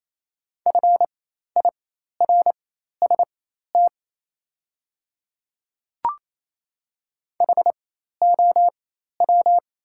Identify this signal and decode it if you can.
Morse code, heard as a keyed tone, about 28 wpm, spaced as for 13 wpm, 700 Hz. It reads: FIRST HOW
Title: 4x Character Spacing - Top 100 Words - 28wpm